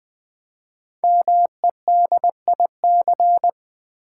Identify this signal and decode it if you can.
Morse code, heard as a keyed tone, about 20 wpm, 700 Hz. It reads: MEDIC
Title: Build A Word - 5 Letters At A Time - 10-Letter Words - 20wpm